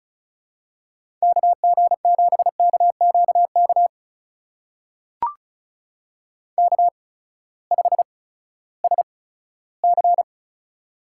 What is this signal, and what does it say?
Morse code, heard as a keyed tone, about 36 wpm, spaced as for 18 wpm, 700 Hz. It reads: KG7KQK K5SC
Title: USA Call Signs - 8x Character Spacing - 35wpm